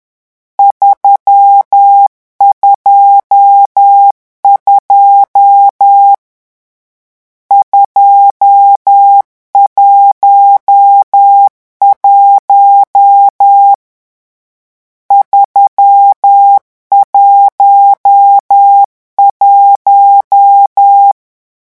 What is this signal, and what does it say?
Morse code, heard as a keyed tone, about 10 wpm, 800 Hz. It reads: 322 211 311
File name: morse mp3 / lecon10-vitesse 10